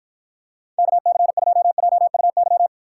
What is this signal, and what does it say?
Morse code, heard as a keyed tone, about 53 wpm, 700 Hz. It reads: KC2JUY